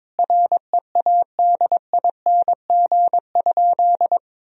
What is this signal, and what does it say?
Morse code, heard as a keyed tone, about 22 wpm, 700 Hz. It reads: READING?
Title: Sentences from Top 1000 Words - Review Part 3 - 22wpm